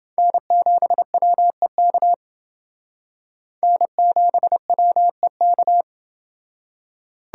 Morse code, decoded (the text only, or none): N7WEK N7WEK